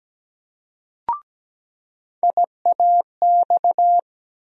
Morse code, heard as a keyed tone, about 17 wpm, 700 Hz. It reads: IAX
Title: USA Call Sign Postfixes - 3 Letters 17wpm